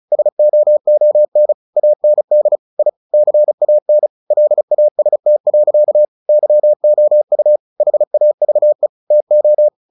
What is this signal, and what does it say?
Morse code, heard as a keyed tone, about 34 wpm, 600 Hz. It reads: SOON AND I CAN LAST. YOU HAVE TO
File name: the old man and the sea 2 35wpm